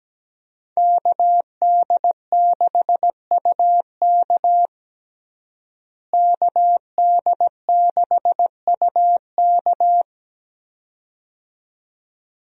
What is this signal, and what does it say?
Morse code, heard as a keyed tone, about 17 wpm, 700 Hz. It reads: KD6UK KD6UK
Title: USA Call Signs - 5 Characters - Repeated 2x - 17wpm